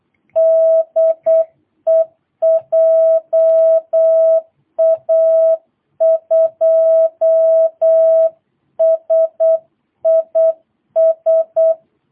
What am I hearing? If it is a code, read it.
Morse code, heard as a keyed tone, about 8 wpm, 650 Hz. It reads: DE JA2SIS
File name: jh8cbh / 2022cw